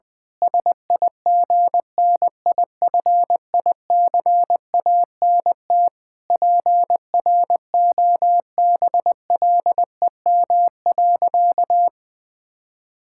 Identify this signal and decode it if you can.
Morse code, heard as a keyed tone, about 20 wpm, 700 Hz. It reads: SIGNIFICANT PROBLEM.
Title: Sentences from Top 1000 Words - Review Part 6 - 20wpm